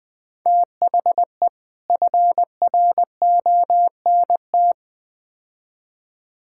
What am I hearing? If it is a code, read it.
Morse code, heard as a keyed tone, about 20 wpm, 700 Hz. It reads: THE FRONT
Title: Build A Phrase - 4-Word Phrases - 20wpm